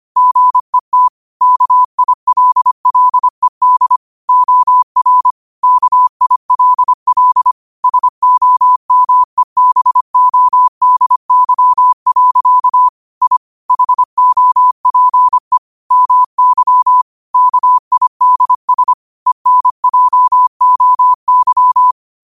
Morse code, heard as a keyed tone, about 25 wpm, 1000 Hz. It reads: GET KILLED OR KILL SOMEBODY. I HOPE MY KIDS ENJOY
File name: 25wpm